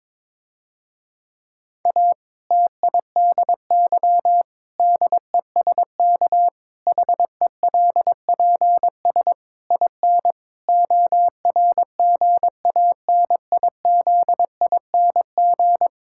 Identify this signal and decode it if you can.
Morse code, heard as a keyed tone, about 21 wpm, 700 Hz. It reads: A TIDY DESK HELPS IN ORGANIZING
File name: Difficult Sentences with Hints - 22wpm